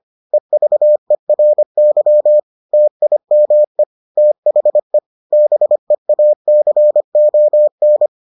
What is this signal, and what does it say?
Morse code, heard as a keyed tone, about 25 wpm, 600 Hz. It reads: EVERY TIME THE BEACON